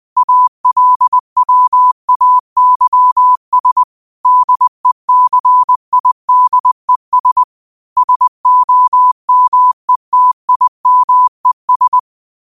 Morse code, heard as a keyed tone, about 20 wpm, 1000 Hz. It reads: ALWAYS DECIDES SOMETIMES